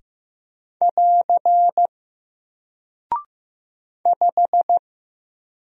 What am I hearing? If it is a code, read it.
Morse code, heard as a keyed tone, about 15 wpm, 700 Hz. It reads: <AR> 5
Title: Single Character (Letter:Number:Punctuation:Prosign) Alt 15wpm